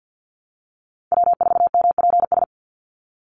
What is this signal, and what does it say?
Morse code, heard as a keyed tone, about 50 wpm, 700 Hz. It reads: W4MPS
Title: NA SOTA Chaser Callsigns 50wpm